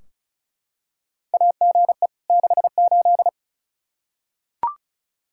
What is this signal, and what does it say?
Morse code, heard as a keyed tone, about 34 wpm, 700 Hz. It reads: AGE 68
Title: Practice QSO - Exchange 3 - More Info - 35wpm